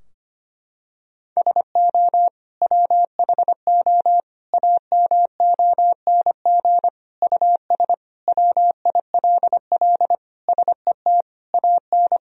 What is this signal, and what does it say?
Morse code, heard as a keyed tone, about 25 wpm, 700 Hz. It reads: SO WHO AMONG US WILL SET AN
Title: Sentences from Top 1000 Words - Review Part 3 - 25wpm